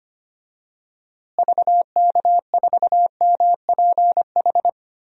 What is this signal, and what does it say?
Morse code, heard as a keyed tone, about 25 wpm, 700 Hz. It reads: VK4MPH
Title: DXCC-100 Call Signs - 6 Characters - Repeated 3x - 25wpm